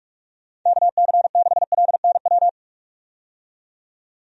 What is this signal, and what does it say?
Morse code, heard as a keyed tone, about 41 wpm, 700 Hz. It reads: KC6LNW
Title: USA Call Signs - 6 Characters - 45wpm